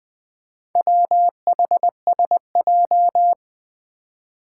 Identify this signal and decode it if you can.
Morse code, heard as a keyed tone, about 20 wpm, 700 Hz. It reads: WHSJ
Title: Four Challenging Letter-Number Combinations 20wpm